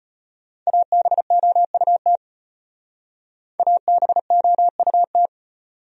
Morse code, heard as a keyed tone, about 33 wpm, 700 Hz. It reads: ABOUT ABOUT